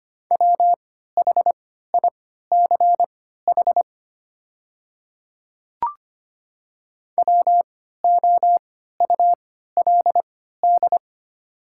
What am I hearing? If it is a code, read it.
Morse code, heard as a keyed tone, about 25 wpm, spaced as for 14 wpm, 690 Hz. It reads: WHICH WOULD
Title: 3x Character Spacing - Top 100 Words - 25wpm